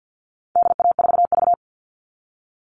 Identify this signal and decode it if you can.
Morse code, heard as a keyed tone, about 50 wpm, 700 Hz. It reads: DA4V
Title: All Cut Numbers - 4-Digit 50wpm